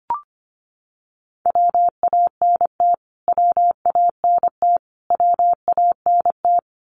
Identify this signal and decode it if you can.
Morse code, heard as a keyed tone, about 25 wpm, 700 Hz. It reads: WANT WANT WANT
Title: Top 500 Words Repeated 25wpm